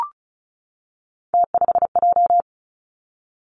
Morse code, heard as a keyed tone, about 36 wpm, 700 Hz. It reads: T5J
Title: Three Challenging Letter-Number Combinations 35wpm